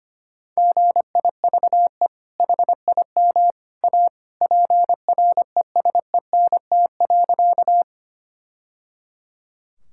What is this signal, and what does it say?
Morse code, heard as a keyed tone, about 25 wpm, 700 Hz. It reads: GIVE HIM A PRESENT.
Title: Sentences from Top 1000 Words - Review Part 5 - 25wpm